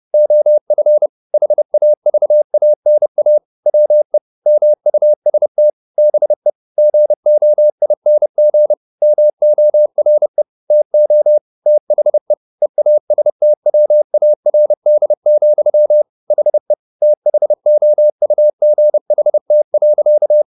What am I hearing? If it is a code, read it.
Morse code, heard as a keyed tone, about 30 wpm, 600 Hz. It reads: OF HAVANA WE MUST BE GOING MORE TO THE EASTWARD, HE THOUGHT.
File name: the old man and the sea / the old man and the sea 1 30wpm